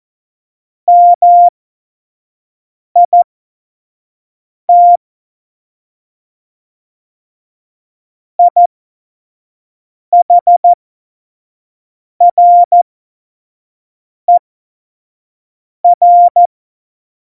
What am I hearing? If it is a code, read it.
Morse code, heard as a keyed tone, about 14 wpm, spaced as for 5 wpm, 700 Hz. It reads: MIT IHRER
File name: Morsekurs / Witze / Witz 19. Februar 2019, 25 F70